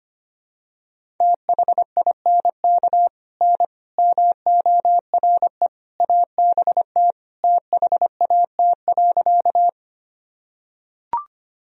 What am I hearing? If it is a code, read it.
Morse code, heard as a keyed tone, about 25 wpm, 700 Hz. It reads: THINK N MORE ABT THAT.
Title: Sentences from Top 200 Words and Abbreviations 25wpm